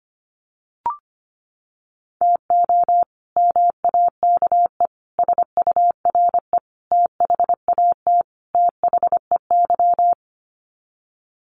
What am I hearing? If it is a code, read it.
Morse code, heard as a keyed tone, about 25 wpm, 700 Hz. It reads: TO MAKE SURE THAT THEY
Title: Sets of 5 Words Encore2 25wpm